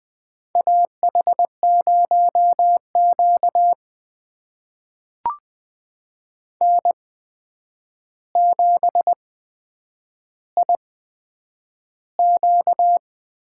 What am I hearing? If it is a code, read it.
Morse code, heard as a keyed tone, about 20 wpm, spaced as for 9 wpm, 700 Hz. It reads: AH0Q N7IQ
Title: USA Call Signs - 4 Characters - 8x Character Spacing - 20wpm